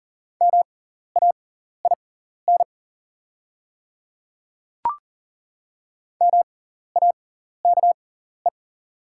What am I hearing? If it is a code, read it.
Morse code, heard as a keyed tone, about 40 wpm, spaced as for 14 wpm, 690 Hz. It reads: MAIN MAKE
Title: 6x Character Spacing - Top 500 Words - 40wpm